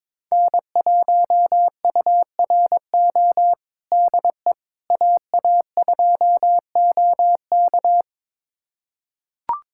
Morse code, heard as a keyed tone, about 21 wpm, 700 Hz. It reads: N1URO DE AA2OK
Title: USA Callsign DE Callsign - 5 Characters - 22wpm